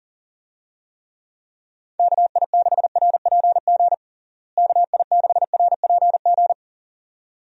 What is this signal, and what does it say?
Morse code, heard as a keyed tone, about 40 wpm, 700 Hz. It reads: KI6RPG KI6RPG
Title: USA Call Signs - 6 Characters - Repeated 2x - 40wpm